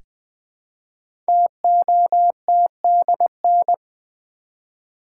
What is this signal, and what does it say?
Morse code, heard as a keyed tone, about 20 wpm, 700 Hz. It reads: TOTDN
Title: Code Groups - 5 Letters - 20wpm